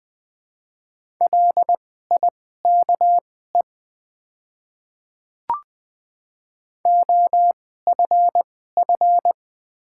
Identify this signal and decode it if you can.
Morse code, heard as a keyed tone, about 20 wpm, spaced as for 15 wpm, 700 Hz. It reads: LIKE OFF
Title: 2x Character Spacing - Top 200 Words - 20wpm